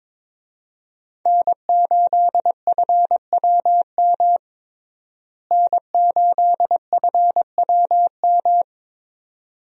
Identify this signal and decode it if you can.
Morse code, heard as a keyed tone, about 21 wpm, 700 Hz. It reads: N8FWM N8FWM
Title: USA Call Signs - 5 Characters - Repeated 2x - 22wpm